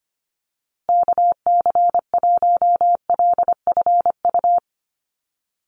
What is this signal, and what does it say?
Morse code, heard as a keyed tone, about 25 wpm, 700 Hz. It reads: KC1LFU
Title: USA Call Signs Repeated 2x 25wpm